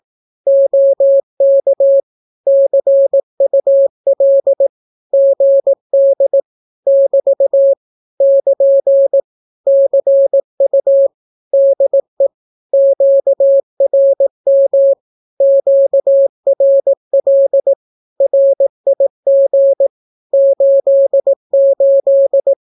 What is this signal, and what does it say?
Morse code, heard as a keyed tone, about 18 wpm, 550 Hz. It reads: OK CUL GD <BT> <KN> CU DE QRM QRL RIG 88
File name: naukapodstaw / skroty / skroty